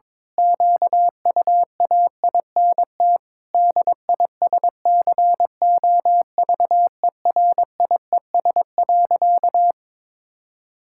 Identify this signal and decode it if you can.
Morse code, heard as a keyed tone, about 23 wpm, 700 Hz. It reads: QUAINT DISCOVERIES.
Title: Pangram Challenge v2 - 22wpm